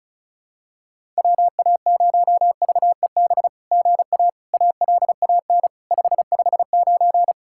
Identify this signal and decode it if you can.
Morse code, heard as a keyed tone, about 33 wpm, 700 Hz. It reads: WA0VEB GA ALAN 559
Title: SOTA QSO Practice 35wpm